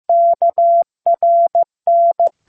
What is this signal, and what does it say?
Morse code, heard as a keyed tone, about 15 wpm, 680 Hz. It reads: KRN